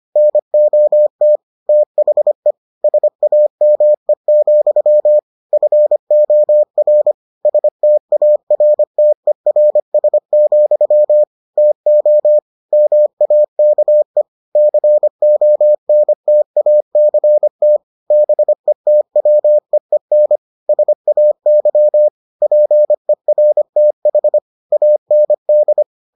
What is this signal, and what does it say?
Morse code, heard as a keyed tone, about 25 wpm, 600 Hz. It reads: NOT THE SAME, FOR STARTERS, TO MAKE CONTACT BETWEEN SAY PERTH AND